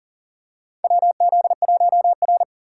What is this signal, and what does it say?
Morse code, heard as a keyed tone, about 37 wpm, 700 Hz. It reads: WZ1R
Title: USA Call Signs - 6x Character Spacing - 40wpm